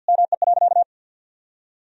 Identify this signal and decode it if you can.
Morse code, heard as a keyed tone, about 50 wpm, 700 Hz. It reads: ME.
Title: Sentences from Top 1000 Words - Review Part 3 - 50wpm